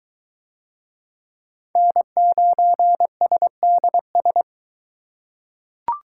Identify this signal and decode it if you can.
Morse code, heard as a keyed tone, about 23 wpm, 700 Hz. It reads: N9SDS